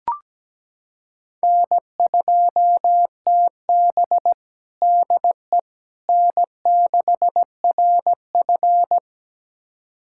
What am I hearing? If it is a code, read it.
Morse code, heard as a keyed tone, about 17 wpm, 700 Hz. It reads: N2TB DE N6RF
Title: USA Callsign DE Callsign - 4 Characters - 17wpm